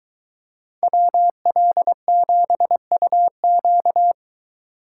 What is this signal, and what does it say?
Morse code, heard as a keyed tone, about 23 wpm, 700 Hz. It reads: WL7UQ